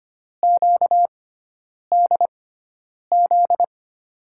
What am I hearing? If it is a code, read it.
Morse code, heard as a keyed tone, about 25 wpm, spaced as for 9 wpm, 700 Hz. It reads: QDZ
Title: USA Call Signs - 6 Characters - 6x Character Spacing - 25wpm